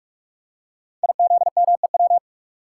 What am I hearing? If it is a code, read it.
Morse code, heard as a keyed tone, about 44 wpm, 700 Hz. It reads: IZMEW
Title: Code Groups - 5 Letters - 45wpm